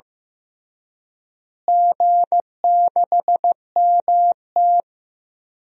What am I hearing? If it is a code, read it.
Morse code, heard as a keyed tone, about 15 wpm, 700 Hz. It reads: G6MT
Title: DXCC-100 Call Signs - 4 Characters - Repeated 3x - 15wpm